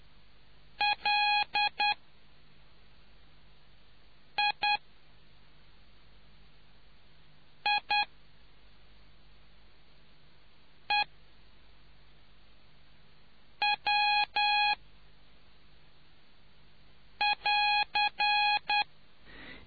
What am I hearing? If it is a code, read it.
Morse code, heard as a keyed tone, about 10 wpm, spaced as for 3 wpm, 830 Hz. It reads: LIIEW<AR>